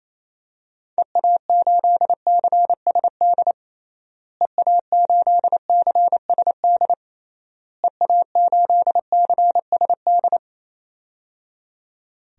Morse code, heard as a keyed tone, about 28 wpm, 700 Hz. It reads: EA8CSD EA8CSD EA8CSD
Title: DXCC-100 Call Signs - 6 Characters - Repeated 3x - 28wpm